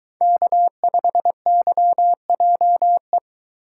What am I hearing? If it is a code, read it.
Morse code, heard as a keyed tone, about 22 wpm, 700 Hz. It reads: K5YJE